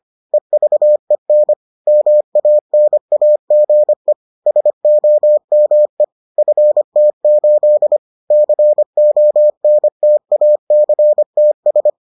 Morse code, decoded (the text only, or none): EVEN MANAGE SOME FT8 CONTACTS